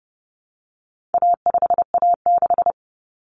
Morse code, heard as a keyed tone, about 30 wpm, 700 Hz. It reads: A5A6
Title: Common Cut Numbers - 4-Digit 30wpm